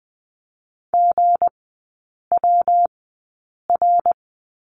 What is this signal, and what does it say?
Morse code, heard as a keyed tone, about 20 wpm, spaced as for 8 wpm, 690 Hz. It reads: GWR
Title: 159 Lesson 26e - Letter g - Three Characters 20wpm